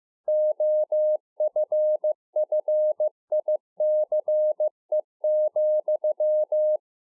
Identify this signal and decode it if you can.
Morse code, heard as a keyed tone, about 15 wpm, 610 Hz. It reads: OFFICE,
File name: 15WPM